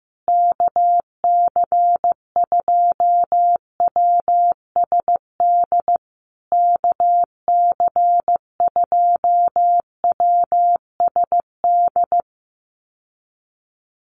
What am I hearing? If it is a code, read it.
Morse code, heard as a keyed tone, about 15 wpm, 700 Hz. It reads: KC2WSD KC2WSD
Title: USA Call Signs Repeated 3x 15wpm